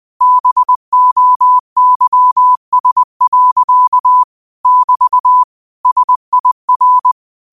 Morse code, heard as a keyed tone, about 20 wpm, 1000 Hz. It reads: BOYS. <BT> SIR
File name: qotdfiles / 20wpm